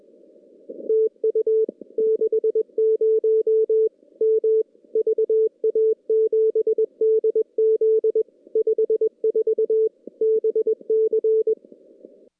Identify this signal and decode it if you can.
Morse code, heard as a keyed tone, about 21 wpm, 440 Hz. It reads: TU 60 M VA7DZ 54 BC
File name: contests / CW-SS-k5tr- -2300